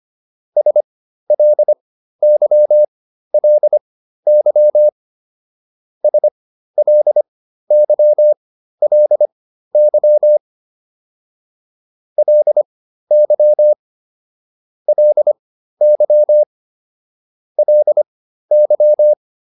Morse code, heard as a keyed tone, about 25 wpm, spaced as for 13 wpm, 600 Hz. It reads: SLYLY SLYLY LY LY LY